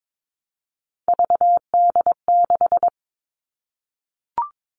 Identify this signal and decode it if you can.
Morse code, heard as a keyed tone, about 22 wpm, 700 Hz. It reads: VD6
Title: All Cut Numbers - 3-Digit 22wpm